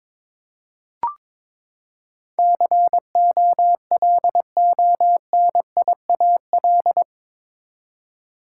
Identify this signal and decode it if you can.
Morse code, heard as a keyed tone, about 22 wpm, 700 Hz. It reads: COLONIAL
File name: practice / 22wpm / Eight Letter Words - 22wpm